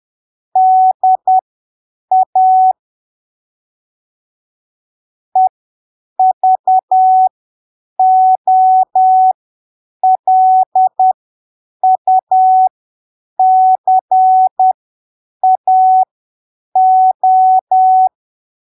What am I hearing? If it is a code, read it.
Morse code, heard as a keyed tone, about 10 wpm, spaced as for 7 wpm, 750 Hz. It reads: DA EVOLUCAO
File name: arquivos / texto / texto 05 10ppm